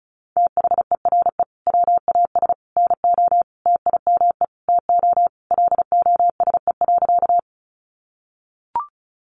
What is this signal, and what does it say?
Morse code, heard as a keyed tone, about 34 wpm, 700 Hz. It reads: THERE WAS NO TIME TO LOSE.
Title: Sentences from Top 1100 Words 35wpm